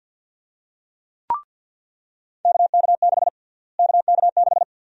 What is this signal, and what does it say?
Morse code, heard as a keyed tone, about 49 wpm, 700 Hz. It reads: KK6 KK6
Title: USA Call Sign Prefixes - Repeated 2x - 50wpm